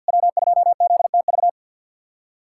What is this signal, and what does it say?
Morse code, heard as a keyed tone, about 52 wpm, 700 Hz. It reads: W2ZTV
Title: USA Callsign DE Callsign 50wpm